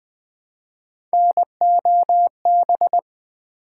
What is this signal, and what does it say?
Morse code, heard as a keyed tone, about 20 wpm, 700 Hz. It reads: NOB
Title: Build A Word - 1 Letter At A Time - 5-Letter Words - 20wpm